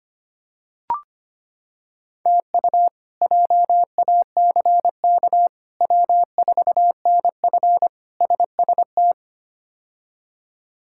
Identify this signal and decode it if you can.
Morse code, heard as a keyed tone, about 25 wpm, 700 Hz. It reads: TU JACK W4NF SST
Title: K1USN SST QSO Practice - 25wpm